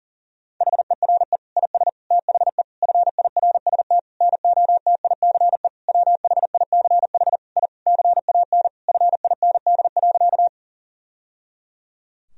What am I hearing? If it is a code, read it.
Morse code, heard as a keyed tone, about 37 wpm, 700 Hz. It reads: HERE IS THE FIRST NOTICE WHICH I CAN FIND.
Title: Sentences from Top 1000 Words - Review Part 2 - 40wpm